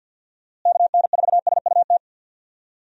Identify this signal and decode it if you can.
Morse code, heard as a keyed tone, about 50 wpm, 700 Hz. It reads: KN4SUT
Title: USA Call Signs - 6 Characters - Repeated 2x - 50wpm